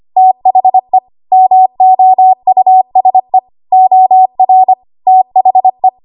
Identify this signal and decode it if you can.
Morse code, heard as a keyed tone, about 25 wpm, 750 Hz. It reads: THE MOUSE OR THE
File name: Archive / 25 WPM / 25WPM